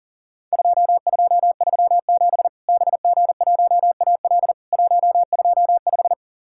Morse code, heard as a keyed tone, about 40 wpm, 700 Hz. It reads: 2237 BG1AL 125
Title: Worked All Europe DX Contest - QTC Exchange - 40wpm